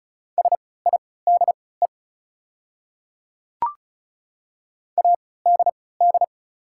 Morse code, heard as a keyed tone, about 34 wpm, spaced as for 20 wpm, 700 Hz. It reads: SIDE ADD
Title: 3x Character Spacing - Top 200 Words - 35wpm